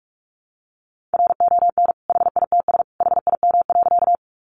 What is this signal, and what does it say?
Morse code, heard as a keyed tone, about 44 wpm, 700 Hz. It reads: RON HITS HIM.
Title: Additional Easy Phrases 45wpm